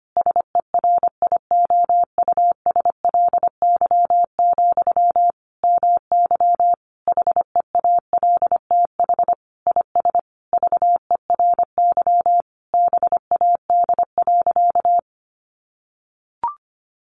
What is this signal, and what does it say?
Morse code, heard as a keyed tone, about 25 wpm, 700 Hz. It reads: SERIOUSLY, MY HEALTH IS VERY BAD.
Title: Sentences from Top 1200 Words 25wpm